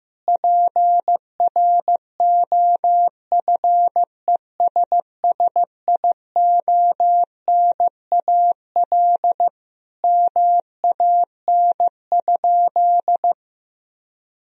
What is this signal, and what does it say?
Morse code, heard as a keyed tone, about 15 wpm, 700 Hz. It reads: PROFESSIONAL MAN?
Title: Sentences from Top 1000 Words - Review Part 4 - 15wpm